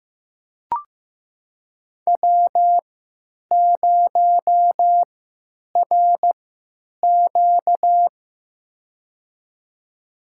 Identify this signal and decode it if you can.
Morse code, heard as a keyed tone, about 15 wpm, spaced as for 9 wpm, 700 Hz. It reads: W0RQ
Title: USA Call Signs - 4 Characters - 3x Character Spacing - 15wpm